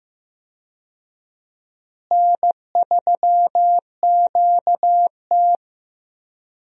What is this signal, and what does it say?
Morse code, heard as a keyed tone, about 15 wpm, 700 Hz. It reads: N3QT
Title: USA Call Signs - 4 Characters - Repeated 3x - 15wpm